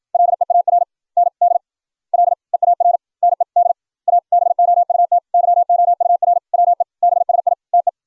Morse code, heard as a keyed tone, about 56 wpm, spaced as for 43 wpm, 700 Hz. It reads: LEARND LEARNED ABOUT YOUR WEBSITE